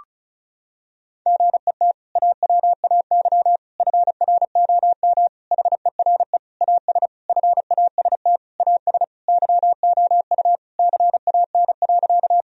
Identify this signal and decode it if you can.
Morse code, heard as a keyed tone, about 34 wpm, 700 Hz. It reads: GET AWAY FROM HERE AS FAST AS YOU CAN.
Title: Sentences from Top 1000 Words - Review Part 2 - 35wpm